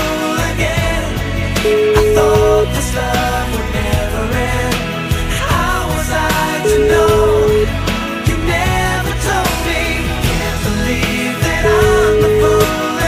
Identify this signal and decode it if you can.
Ringback tone (Europe, 425 Hz), with noise.